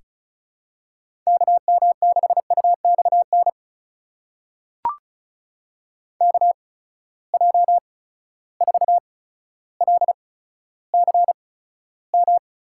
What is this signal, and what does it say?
Morse code, heard as a keyed tone, about 36 wpm, spaced as for 15 wpm, 700 Hz. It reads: KM6UXN KJ4LCM
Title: USA Call Signs - 6 Characters - 8x Character Spacing - 35wpm